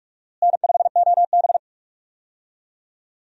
Morse code, heard as a keyed tone, about 41 wpm, 700 Hz. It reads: N5OB
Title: USA Call Signs - 4 Characters - Repeated 2x - 45wpm